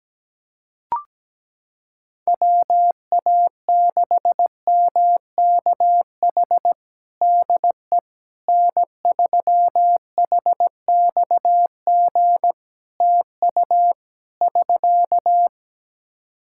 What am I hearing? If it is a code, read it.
Morse code, heard as a keyed tone, about 17 wpm, 700 Hz. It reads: WA6MKH DE N3HXG TU <SK>
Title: Practice QSO - Exchange 4 - Ending - 17wpm